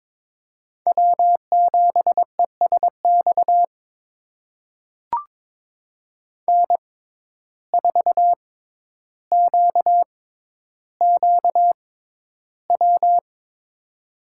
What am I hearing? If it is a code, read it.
Morse code, heard as a keyed tone, about 22 wpm, spaced as for 11 wpm, 700 Hz. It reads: W7ESX N4QQW
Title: USA Call Signs - 5 Characters - 6x Character Spacing - 22wpm